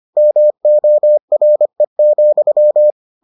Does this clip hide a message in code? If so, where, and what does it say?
Yes: Morse code from 0.2 to 2.9 s, heard as a keyed tone, about 25 wpm, 600 Hz. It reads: MORE,